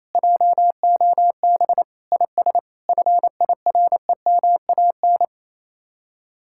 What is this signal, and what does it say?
Morse code, heard as a keyed tone, about 37 wpm, 700 Hz. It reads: JOB IS FIREMAN